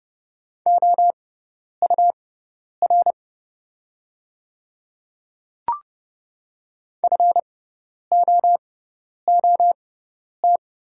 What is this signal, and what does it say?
Morse code, heard as a keyed tone, about 30 wpm, spaced as for 10 wpm, 690 Hz. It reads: OUR FOOT
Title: 6x Character Spacing - Top 500 Words - 30wpm